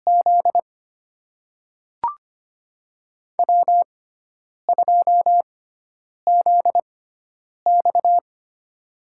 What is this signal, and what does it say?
Morse code, heard as a keyed tone, about 25 wpm, spaced as for 9 wpm, 700 Hz. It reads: Z W2ZX
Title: USA Call Signs - 4 Characters - 6x Character Spacing - 25wpm